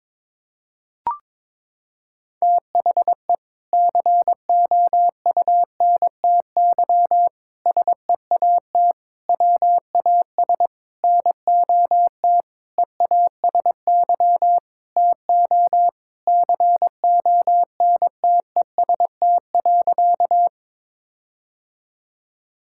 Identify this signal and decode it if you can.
Morse code, heard as a keyed tone, about 22 wpm, 700 Hz. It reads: THE COUNTY SEAT WAS NOT EASY TO CONTEST.
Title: Sentences from Top 1600 Words 22wpm